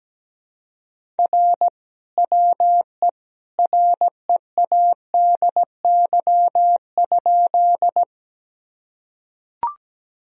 Morse code, heard as a keyed tone, about 17 wpm, 700 Hz. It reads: R WE READY?
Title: Sentences from Top 400 Words and Abbreviations 17wpm